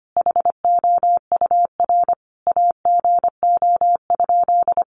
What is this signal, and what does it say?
Morse code, heard as a keyed tone, about 25 wpm, 700 Hz. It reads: HOUR AGO?
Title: Sentences from Top 1300 Words 25wpm